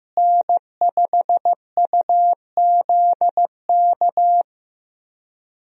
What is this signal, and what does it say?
Morse code, heard as a keyed tone, about 15 wpm, 700 Hz. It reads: N5UZK
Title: USA Call Signs - 5 Characters - Repeated 2x - 15wpm